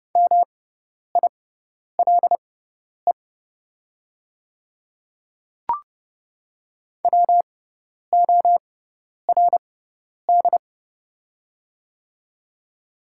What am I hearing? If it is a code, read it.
Morse code, heard as a keyed tone, about 30 wpm, spaced as for 10 wpm, 690 Hz. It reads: MILE WORD
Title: 6x Character Spacing - Top 500 Words - 30wpm